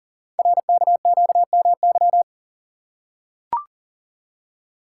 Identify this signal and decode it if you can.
Morse code, heard as a keyed tone, about 41 wpm, 690 Hz. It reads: RKQMY
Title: Code Groups - 5 Letter:Numbers - 40wpm